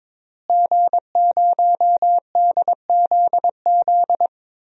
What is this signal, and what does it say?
Morse code, heard as a keyed tone, about 22 wpm, 700 Hz. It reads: G0DZZ
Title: DXCC-100 Call Signs - 5 Characters - Repeated 3x - 22wpm